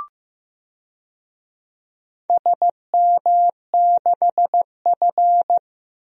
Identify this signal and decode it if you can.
Morse code, heard as a keyed tone, about 15 wpm, 700 Hz. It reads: SM6F
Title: DXCC-100 Call Signs - 4 Characters - 15wpm